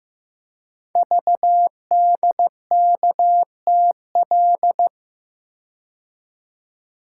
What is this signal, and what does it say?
Morse code, heard as a keyed tone, about 15 wpm, 700 Hz. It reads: VDKTL
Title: Code Groups - 5 Letters - 15wpm